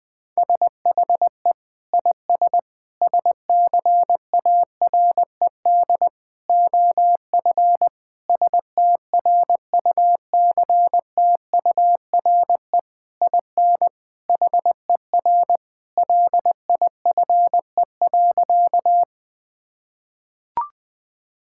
Morse code, heard as a keyed tone, about 20 wpm, 700 Hz. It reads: SHE IS SCARED OF STRUCTURE IN HER LIFE.
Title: Sentences from Top 1000 Words - Review Part 6 - 20wpm